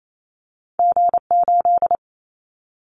Morse code, heard as a keyed tone, about 28 wpm, 700 Hz. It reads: G8
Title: Callsign Prefixes for Top 25 DXCC - Repeated 3x - 28wpm